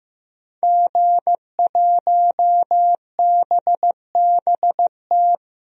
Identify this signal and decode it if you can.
Morse code, heard as a keyed tone, about 15 wpm, 700 Hz. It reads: G1BBT
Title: DXCC-100 Call Signs - 5 Characters - 15wpm